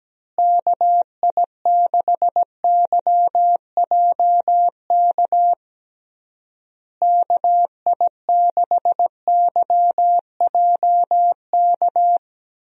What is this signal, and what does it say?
Morse code, heard as a keyed tone, about 17 wpm, 700 Hz. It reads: KI6YJK KI6YJK
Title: USA Call Signs - 6 Characters - Repeated 3x - 17wpm